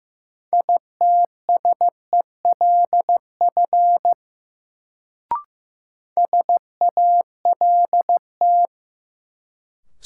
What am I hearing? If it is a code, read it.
Morse code, heard as a keyed tone, about 15 wpm, 700 Hz. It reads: ITSELF SALT
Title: Sentences from Top 800 Words - New Word Drill - 15wpm